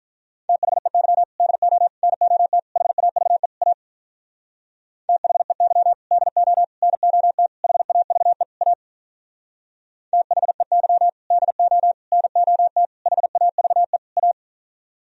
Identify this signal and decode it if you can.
Morse code, heard as a keyed tone, about 43 wpm, 700 Hz. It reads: THEY DO NOT HAVE A THEY DO NOT HAVE A THEY DO NOT HAVE A
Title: Speed-Racing - Sets of 5 Words 35wpm